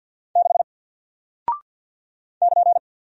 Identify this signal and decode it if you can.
Morse code, heard as a keyed tone, about 48 wpm, 700 Hz. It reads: 6 <KN>
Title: Single Character (Letter:Number:Punctuation:Prosign) Alt 50wpm